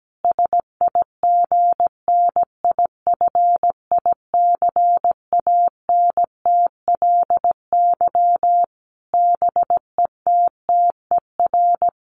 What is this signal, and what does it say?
Morse code, heard as a keyed tone, about 17 wpm, 700 Hz. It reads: SIGNIFICANTLY BETTER
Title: Sentences from Top 1500 Words 17wpm